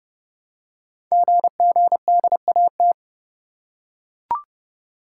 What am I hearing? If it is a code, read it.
Morse code, heard as a keyed tone, about 30 wpm, 700 Hz. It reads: GGDAT
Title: Code Groups - 5 Letters - 30wpm